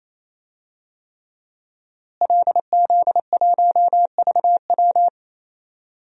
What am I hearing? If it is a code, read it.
Morse code, heard as a keyed tone, about 27 wpm, 700 Hz. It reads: LZ1VW
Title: DXCC-100 Call Signs - 5 Characters - Repeated 3x - 28wpm